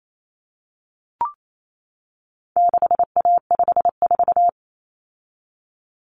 Morse code, heard as a keyed tone, about 28 wpm, 700 Hz. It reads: 6A54
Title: Common Cut Numbers - 4-Digit 28wpm